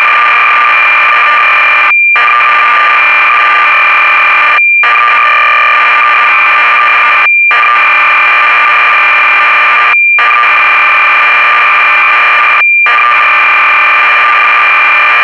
Cassette data (Kansas City type).